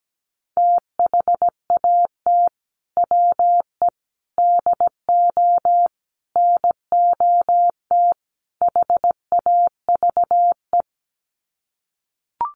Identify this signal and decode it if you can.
Morse code, heard as a keyed tone, about 17 wpm, 690 Hz. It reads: THAT WE DO NOT HAVE
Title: Sets of 5 Words Encore 17wpm